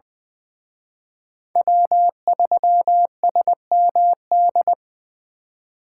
Morse code, heard as a keyed tone, about 23 wpm, 700 Hz. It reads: W3SMD
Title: Speed Racing - USA 1x3 Call Signs 20wpm